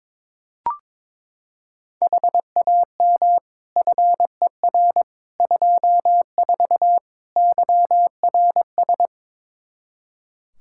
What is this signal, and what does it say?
Morse code, heard as a keyed tone, about 21 wpm, 700 Hz. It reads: HAM FER 24 YRS
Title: Practice QSO - Exchange 3 - More Info - 22wpm